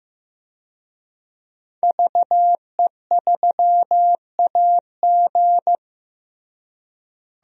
Morse code, heard as a keyed tone, about 15 wpm, 700 Hz. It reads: VE3AG
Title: DXCC-100 Call Signs - 5 Characters - Repeated 2x - 15wpm